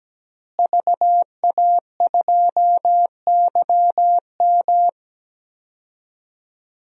Morse code, heard as a keyed tone, about 17 wpm, 700 Hz. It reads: VA2YM
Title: DXCC-100 Call Signs - 5 Characters - Repeated 3x - 17wpm